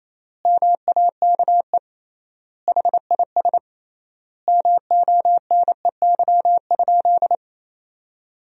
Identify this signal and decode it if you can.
Morse code, heard as a keyed tone, about 28 wpm, 700 Hz. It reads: MAKE HIS MONEY?
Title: 3x Word-Spacing - Sentences from Top 400 Words - 28wpm